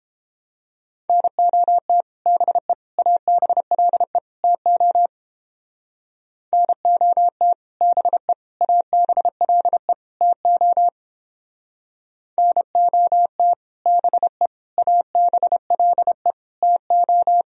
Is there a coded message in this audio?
Yes: Morse code from 1.1 to 17.4 s, heard as a keyed tone, about 29 wpm, 700 Hz. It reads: NOT BE ABLE TO NOT BE ABLE TO NOT BE ABLE TO